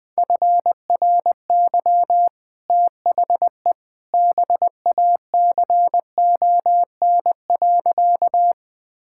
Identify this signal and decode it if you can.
Morse code, heard as a keyed tone, about 20 wpm, 700 Hz. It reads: FRY THE BACON.
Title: Easy Sentences with Hints - 20wpm